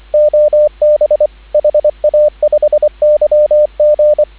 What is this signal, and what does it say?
Morse code, heard as a keyed tone, about 25 wpm, 600 Hz, with radio noise. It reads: OB HA5YG